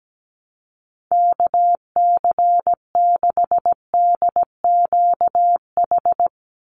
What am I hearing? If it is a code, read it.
Morse code, heard as a keyed tone, about 17 wpm, 700 Hz. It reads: KC6DQH
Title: USA Call Signs 17wpm